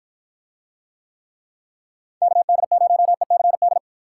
Morse code, heard as a keyed tone, about 48 wpm, 700 Hz. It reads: KD0ECD